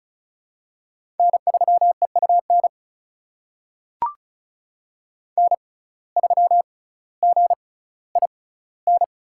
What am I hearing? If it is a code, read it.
Morse code, heard as a keyed tone, about 36 wpm, spaced as for 18 wpm, 700 Hz. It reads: N3EUN N3GIN
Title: USA Call Signs - 5 Characters - 6x Character Spacing - 35wpm